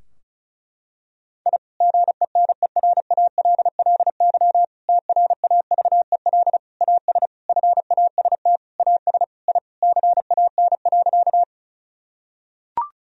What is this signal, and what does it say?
Morse code, heard as a keyed tone, about 34 wpm, 700 Hz. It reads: I GENERALLY TRAVEL AS FAST AS I CAN.
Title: Sentences from Top 1000 Words - Review Part 4 - 35wpm